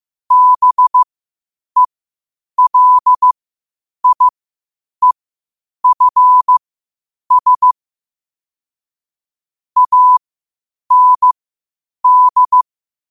Morse code, heard as a keyed tone, about 15 wpm, spaced as for 8 wpm, 1000 Hz. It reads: BELIEFS AND
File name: qotdfiles / 8wpm